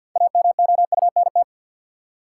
Morse code, heard as a keyed tone, about 53 wpm, 700 Hz. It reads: AMOUNT
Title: Six Letter Words - 50wpm